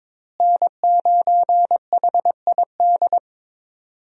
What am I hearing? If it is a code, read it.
Morse code, heard as a keyed tone, about 22 wpm, 700 Hz. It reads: N9HID
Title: USA Call Signs - 5 Characters - Repeated 2x - 22wpm